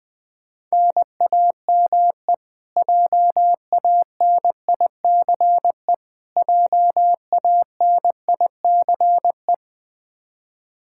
Morse code, heard as a keyed tone, about 21 wpm, 700 Hz. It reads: NAME JANICE JANICE
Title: Practice QSO - Exchange 1 - Beginning - 20wpm